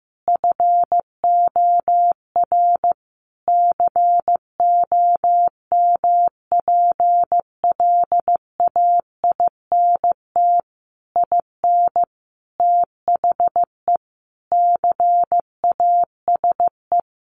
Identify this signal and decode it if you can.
Morse code, heard as a keyed tone, about 15 wpm, 700 Hz. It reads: FOR COMPLAINT IN THE CASE
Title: Sentences from Top 1400 Words 15wpm